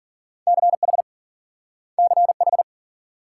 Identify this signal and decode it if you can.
Morse code, heard as a keyed tone, about 53 wpm, 700 Hz. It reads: CH CH